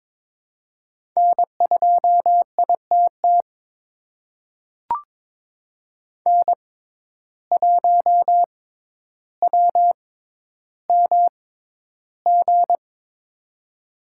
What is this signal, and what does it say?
Morse code, heard as a keyed tone, about 22 wpm, spaced as for 11 wpm, 700 Hz. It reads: N2ITT N1WMG
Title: USA Call Signs - 5 Characters - 6x Character Spacing - 22wpm